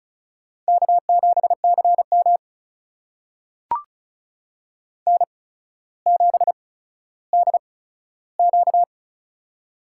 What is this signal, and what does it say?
Morse code, heard as a keyed tone, about 36 wpm, spaced as for 15 wpm, 700 Hz. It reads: K7CM N7DQ
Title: USA Call Signs - 4 Characters - 8x Character Spacing - 35wpm